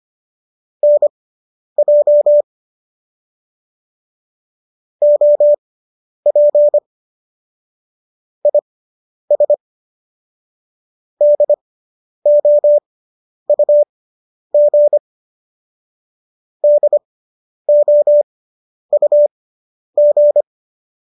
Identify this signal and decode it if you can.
Morse code, heard as a keyed tone, about 25 wpm, spaced as for 10 wpm, 590 Hz. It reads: NJ OP IS DOUG DOUG